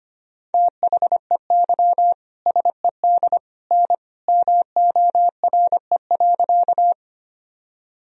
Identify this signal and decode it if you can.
Morse code, heard as a keyed tone, about 25 wpm, 700 Hz. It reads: THEY SED N MORE.
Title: Sentences from Top 100 Words and Abbreviations 25wpm